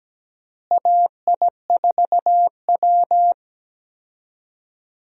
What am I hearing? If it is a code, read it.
Morse code, heard as a keyed tone, about 17 wpm, 700 Hz. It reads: AI4W
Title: USA Call Signs - 4 Characters - Repeated 2x - 17wpm